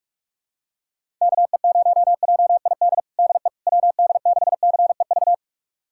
Morse code, heard as a keyed tone, about 45 wpm, 700 Hz. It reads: KE0JID DE WD6CEV
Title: USA Callsign DE Callsign - 6 Characters - 45wpm